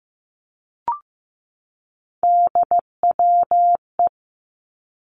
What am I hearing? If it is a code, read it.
Morse code, heard as a keyed tone, about 15 wpm, 700 Hz. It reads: DWE
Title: Three Challenging Letter-Number Combinations 15wpm